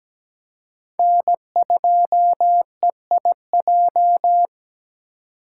Morse code, heard as a keyed tone, about 17 wpm, 700 Hz. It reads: N2EIJ
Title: USA Call Signs - 3x Character Spacing - 17wpm